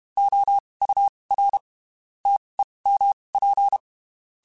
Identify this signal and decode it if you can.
Morse code, heard as a keyed tone, about 32 wpm, spaced as for 23 wpm, 800 Hz. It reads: OUR TEMP
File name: (spaced) / QSOs Spaced-mp3s / WD0FBT @30 sp